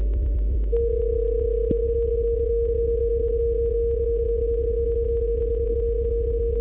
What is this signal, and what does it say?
PSK31, about 31.25 baud.
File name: luxorion / Radio / qso-bpsk31-g4ydo-s51ab- -1408z-14071-6khz